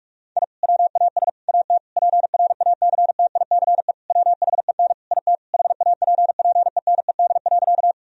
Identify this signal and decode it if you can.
Morse code, heard as a keyed tone, about 46 wpm, 700 Hz. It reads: I WAS AT PRACTICE WHEN IT HAPPENED.
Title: Sentences from Top 1000 Words - Review Part 4 - 45wpm